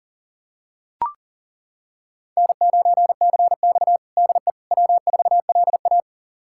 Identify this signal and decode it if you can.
Morse code, heard as a keyed tone, about 40 wpm, 700 Hz. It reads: N9CX DE W4LA
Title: USA Callsign DE Callsign - 4 Characters - 40wpm